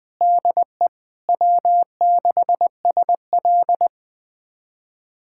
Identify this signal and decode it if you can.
Morse code, heard as a keyed tone, about 20 wpm, 700 Hz. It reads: DE W6SL
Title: USA Callsign DE Callsign - 4 Characters - 20wpm